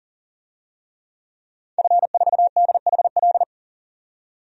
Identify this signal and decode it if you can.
Morse code, heard as a keyed tone, about 40 wpm, 700 Hz. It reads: F4DHL
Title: DXCC-100 Call Signs - 5 Characters - 40wpm